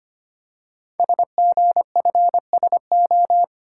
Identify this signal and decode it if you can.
Morse code, heard as a keyed tone, about 25 wpm, 700 Hz. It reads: SGFSO
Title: Code Groups - 5 Letters - 25wpm